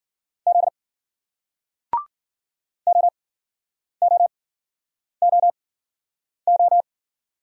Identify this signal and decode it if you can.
Morse code, heard as a keyed tone, about 54 wpm, spaced as for 9 wpm, 700 Hz. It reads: Z OOOO